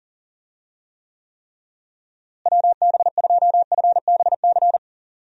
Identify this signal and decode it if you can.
Morse code, heard as a keyed tone, about 40 wpm, 700 Hz. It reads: WB2FBC
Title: USA Call Signs - 6 Characters - Repeated 3x - 40wpm